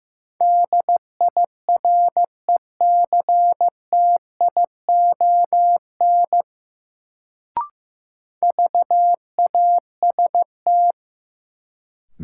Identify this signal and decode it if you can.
Morse code, heard as a keyed tone, about 15 wpm, 700 Hz. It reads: DIRECTION VAST
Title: Sets of 4 Words - New Word Drill - 15wpm